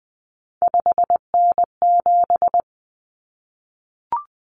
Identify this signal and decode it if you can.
Morse code, heard as a keyed tone, about 20 wpm, 700 Hz. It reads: 5N7
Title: Common Cut Numbers - 3-Digit 20wpm